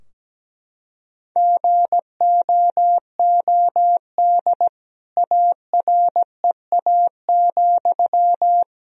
Morse code, heard as a keyed tone, about 17 wpm, 700 Hz. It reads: GOOD AREA,
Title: Sentences from Top 1000 Words - Review Part 2 - 17wpm